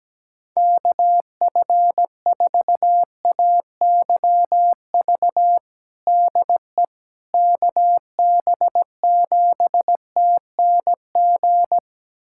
Morse code, heard as a keyed tone, about 17 wpm, 700 Hz. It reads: KF4AYV DE KB7TNG
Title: USA Callsign DE Callsign - 6 Characters - 17wpm